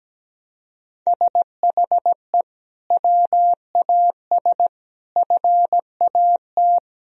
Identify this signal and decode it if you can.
Morse code, heard as a keyed tone, about 17 wpm, 700 Hz. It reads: SHE WAS FAT
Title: Sentences from Top 1000 Words - Review Part 4 - 17wpm